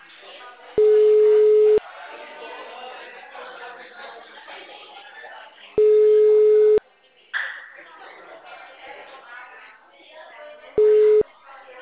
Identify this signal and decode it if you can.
Ringback tone (Europe, 425 Hz), with speech.